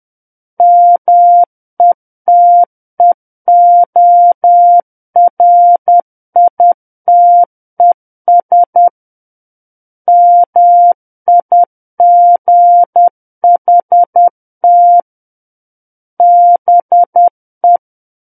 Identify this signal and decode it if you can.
Morse code, heard as a keyed tone, about 10 wpm, 700 Hz. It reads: METEORITES MIGHT BE
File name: Book One / 01-Chapter 1 - 10 WPM